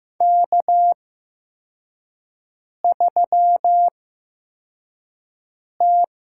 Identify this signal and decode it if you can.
Morse code, heard as a keyed tone, about 15 wpm, spaced as for 4 wpm, 700 Hz. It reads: K3T
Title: USA Call Signs - 5 Characters - 8x Character Spacing - 15wpm